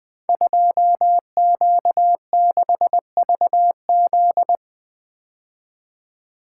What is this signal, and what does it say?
Morse code, heard as a keyed tone, about 20 wpm, 700 Hz. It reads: 2Q6VZ
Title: Code Groups - 5 Letter:Numbers - 20wpm